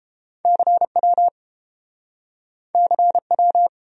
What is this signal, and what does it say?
Morse code, heard as a keyed tone, about 30 wpm, 700 Hz. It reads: CW CW